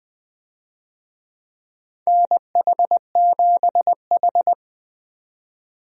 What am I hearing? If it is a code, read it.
Morse code, heard as a keyed tone, about 20 wpm, 700 Hz. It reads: NH7H
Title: USA Call Signs - 4 Characters - 20wpm